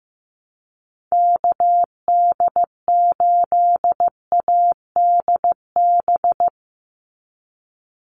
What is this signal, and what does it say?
Morse code, heard as a keyed tone, about 15 wpm, 700 Hz. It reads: KD8ADB
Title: USA Call Signs 15wpm